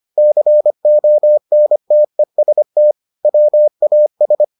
Morse code, heard as a keyed tone, about 25 wpm, 600 Hz. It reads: CONTEST WAS